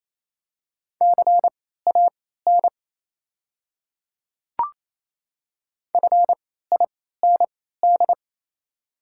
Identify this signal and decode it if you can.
Morse code, heard as a keyed tone, about 28 wpm, spaced as for 16 wpm, 700 Hz. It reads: CAN FIND
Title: 3x Character Spacing - Top 200 Words - 28wpm